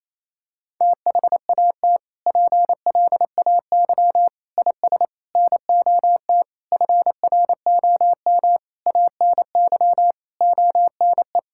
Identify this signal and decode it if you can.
Morse code, heard as a keyed tone, about 28 wpm, 700 Hz. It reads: THAT PLAY IS NOT FROM ANY ONE
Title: Sentences from Top 1000 Words - Review Part 5 - 28wpm